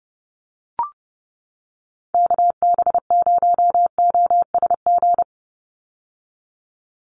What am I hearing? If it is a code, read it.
Morse code, heard as a keyed tone, about 29 wpm, 700 Hz. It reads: KB0OSG
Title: USA Call Signs 30wpm